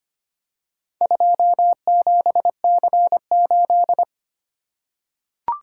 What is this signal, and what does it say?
Morse code, heard as a keyed tone, about 25 wpm, 700 Hz. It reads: 27C8
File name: practice / 25wpm / Four Letter-Number Combinations - 25wpm